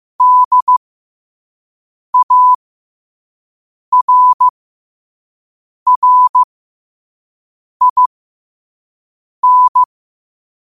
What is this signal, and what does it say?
Morse code, heard as a keyed tone, about 15 wpm, spaced as for 5 wpm, 1000 Hz. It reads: DARRIN
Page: New quotes every day in morse code at 5 Words per minute.